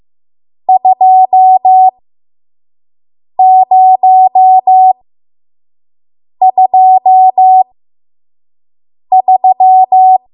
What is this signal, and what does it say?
Morse code, heard as a keyed tone, about 15 wpm, spaced as for 5 wpm, 750 Hz. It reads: 2023
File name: Morse / 05WPM